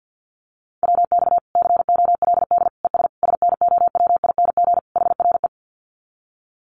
Morse code, heard as a keyed tone, about 50 wpm, 700 Hz. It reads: WX COLD ES SNOWING HRE
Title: Common QSO Phrases 50wpm